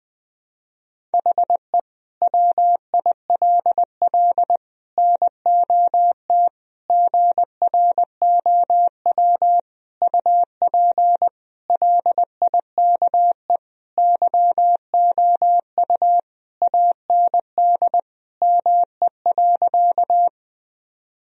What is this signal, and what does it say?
Morse code, heard as a keyed tone, about 21 wpm, 700 Hz. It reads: HE WILL NOT GROW UP LIKE YOU AND ME.
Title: Sentences from Top 1000 Words - Review Part 1 - 20wpm